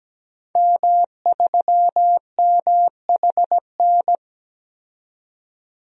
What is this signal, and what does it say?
Morse code, heard as a keyed tone, about 17 wpm, 700 Hz. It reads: M3MHN
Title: DXCC-100 Call Signs - 5 Characters - 17wpm